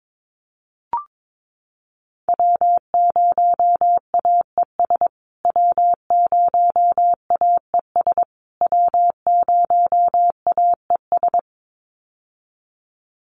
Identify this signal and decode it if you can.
Morse code, heard as a keyed tone, about 22 wpm, 700 Hz. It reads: W0AES W0AES W0AES
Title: USA Call Signs Repeated 3x 22wpm